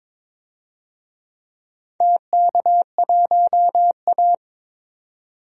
Morse code, heard as a keyed tone, about 21 wpm, 700 Hz. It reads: TK1A
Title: DXCC-100 Call Signs - 4 Characters - 22wpm